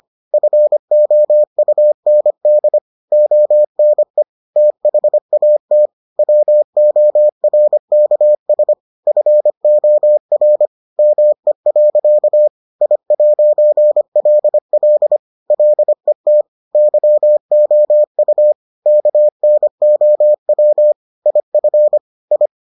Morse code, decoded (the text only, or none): FOUND ONE THAT WORKS FOR ME. I'LL LET YOU KNOW IF I